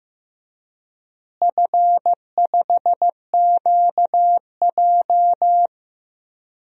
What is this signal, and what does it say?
Morse code, heard as a keyed tone, about 15 wpm, 700 Hz. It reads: F5QJ
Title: DXCC-100 Call Signs - 4 Characters - Repeated 3x - 15wpm